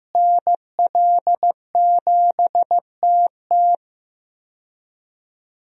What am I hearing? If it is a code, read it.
Morse code, heard as a keyed tone, about 15 wpm, 700 Hz. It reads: NL7TT
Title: USA Call Signs - 5 Characters - Repeated 2x - 15wpm